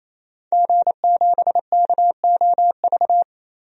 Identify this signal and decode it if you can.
Morse code, heard as a keyed tone, about 28 wpm, 700 Hz. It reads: G7KOV
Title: DXCC-100 Call Signs - 5 Characters - 28wpm